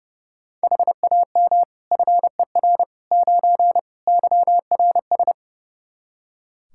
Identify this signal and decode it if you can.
Morse code, heard as a keyed tone, about 30 wpm, 700 Hz. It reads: HAM FER 9 YRS
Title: K1USN SST QSO Practice 30wpm